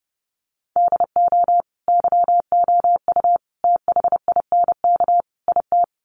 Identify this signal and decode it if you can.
Morse code, heard as a keyed tone, about 30 wpm, 700 Hz. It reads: DO YOU THINK IT
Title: Sets of 4 Words 30wpm